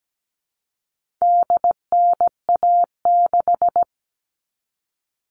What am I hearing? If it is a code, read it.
Morse code, heard as a keyed tone, about 17 wpm, 700 Hz. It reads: DNA6
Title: All Cut Numbers - 4-Digit 17wpm